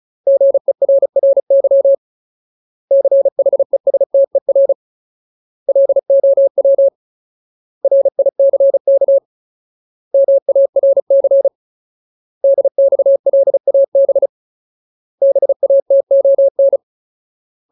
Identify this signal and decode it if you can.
Morse code, heard as a keyed tone, about 34 wpm, 550 Hz. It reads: GERRY CHESTER LOW RICK MARC DXLAB BATON
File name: Ultimate Word List / 35WPM Ultimate